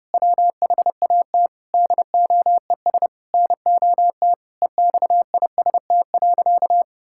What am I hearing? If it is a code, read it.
Morse code, heard as a keyed tone, about 30 wpm, 700 Hz. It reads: WHAT DOES NOT EXIST.
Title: Sentences from Top 1000 Words - Review Part 5 - 30wpm